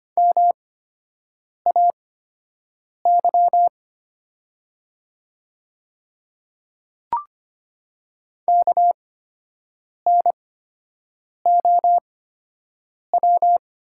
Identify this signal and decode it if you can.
Morse code, heard as a keyed tone, about 25 wpm, spaced as for 7 wpm, 700 Hz. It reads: MAY KNOW
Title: 8x Character Spacing - Top 100 Words - 25wpm